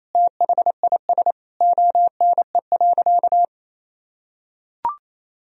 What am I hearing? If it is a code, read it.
Morse code, heard as a keyed tone, about 28 wpm, 700 Hz. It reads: THIS ONE.
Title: Sentences from Top 1000 Words - Review Part 4 - 28wpm